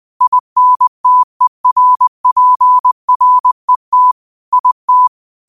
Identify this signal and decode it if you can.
Morse code, heard as a keyed tone, about 20 wpm, 1000 Hz. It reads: INTERPRET IT
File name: qotdfiles / 20wpm